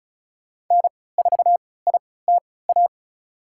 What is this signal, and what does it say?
Morse code, heard as a keyed tone, about 34 wpm, spaced as for 20 wpm, 700 Hz. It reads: N4ITA
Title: USA Call Signs - 5 Characters - 3x Character Spacing - 35wpm